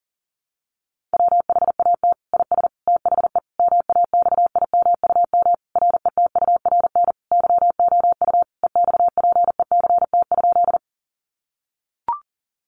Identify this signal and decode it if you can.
Morse code, heard as a keyed tone, about 40 wpm, 700 Hz. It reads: WHAT IS THE MAXIMUM RETURN YOU EXPECT?
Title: Sentences from Top 1300 Words 40wpm